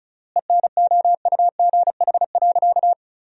Morse code, heard as a keyed tone, about 34 wpm, 700 Hz. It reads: ENOUGH.
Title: 3x Word-Spacing - Sentences from Top 400 Words - 35wpm